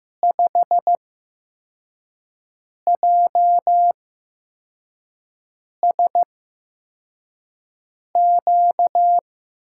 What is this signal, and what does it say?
Morse code, heard as a keyed tone, about 15 wpm, spaced as for 4 wpm, 700 Hz. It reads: 5JSQ
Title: USA Call Signs - 6 Characters - 8x Character Spacing - 15wpm